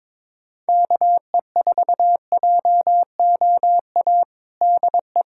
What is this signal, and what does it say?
Morse code, heard as a keyed tone, about 22 wpm, 700 Hz. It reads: KE4JOA DE
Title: USA Callsign DE Callsign - 6 Characters - 22wpm